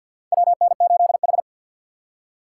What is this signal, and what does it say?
Morse code, heard as a keyed tone, about 52 wpm, 700 Hz. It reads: WN8H
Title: USA Call Signs - 4 Characters - 2x Character Spacing - 50wpm